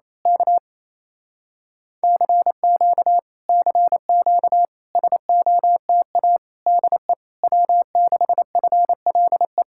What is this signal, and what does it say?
Morse code, heard as a keyed tone, about 26 wpm, 700 Hz. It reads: K CQ CQ SOTA DE W6FLE
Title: Speed-Racing Common QSO Phrases - 28wpm